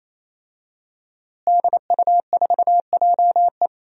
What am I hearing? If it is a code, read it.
Morse code, heard as a keyed tone, about 28 wpm, 700 Hz. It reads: DU4JE
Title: DXCC-100 Call Signs - 5 Characters - Repeated 3x - 28wpm